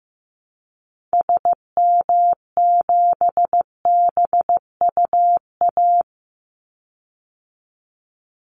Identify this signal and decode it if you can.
Morse code, heard as a keyed tone, about 15 wpm, 700 Hz. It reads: SM7BUA
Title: Call Signs Repeated 2x - 15wpm